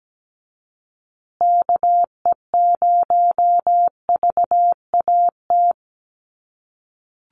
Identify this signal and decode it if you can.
Morse code, heard as a keyed tone, about 17 wpm, 700 Hz. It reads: KE0VAT
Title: USA Call Signs Repeated 2x 17wpm